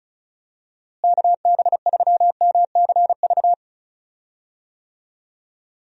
Morse code, heard as a keyed tone, about 36 wpm, 700 Hz. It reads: KB3MCV
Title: USA Call Signs - 6 Characters - Repeated 2x - 35wpm